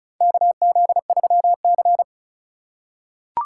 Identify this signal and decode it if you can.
Morse code, heard as a keyed tone, about 34 wpm, 700 Hz. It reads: KZ3C
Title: USA Call Signs - 4 Characters - 4x Character Spacing - 35wpm